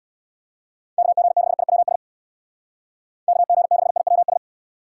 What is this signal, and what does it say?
Morse code, heard as a keyed tone, about 64 wpm, 700 Hz. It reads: KG6IWS KG6IWS